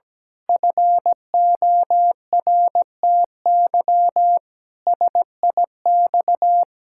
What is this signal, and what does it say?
Morse code, heard as a keyed tone, about 17 wpm, 700 Hz. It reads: FORTY SIX
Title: Pangram Challenge v2 - 17wpm